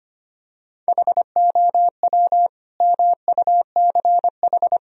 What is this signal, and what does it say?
Morse code, heard as a keyed tone, about 25 wpm, 700 Hz. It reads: HOW MUCH
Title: Sentences from Top 1000 Words - Review Part 3 - 25wpm